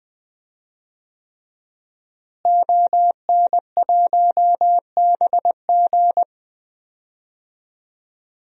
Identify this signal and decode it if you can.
Morse code, heard as a keyed tone, about 20 wpm, 700 Hz. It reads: ON1BG
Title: DXCC-100 Call Signs - 5 Characters - Repeated 3x - 20wpm